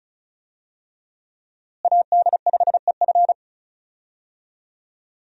Morse code, heard as a keyed tone, about 36 wpm, 700 Hz. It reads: AD5EF
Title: USA Call Signs - 5 Characters - Repeated 2x - 35wpm